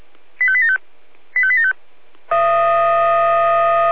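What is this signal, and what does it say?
Five-tone selective call (ZVEI), heard as quick runs of five tones, tones of about 70 ms. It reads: 85875 85875
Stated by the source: ZVEI five-tone call received on SDR, recorded by the Orlamünde fire brigade (Feuerwehr Orlamünde).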